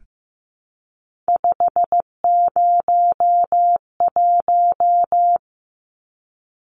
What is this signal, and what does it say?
Morse code, heard as a keyed tone, about 15 wpm, 700 Hz. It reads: 501
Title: Numbers - 3-Digit 15wpm